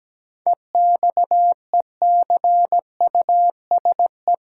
Morse code, heard as a keyed tone, about 17 wpm, 700 Hz. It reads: EXECUSE
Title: Sentences from Top 700 Words - New Word Drill - 17wpm